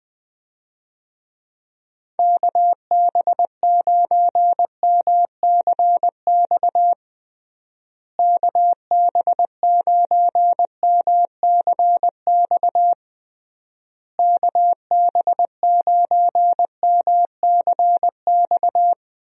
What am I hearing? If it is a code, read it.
Morse code, heard as a keyed tone, about 20 wpm, 700 Hz. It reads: KB9MCX KB9MCX KB9MCX
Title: USA Call Signs - 6 Characters - Repeated 3x - 20wpm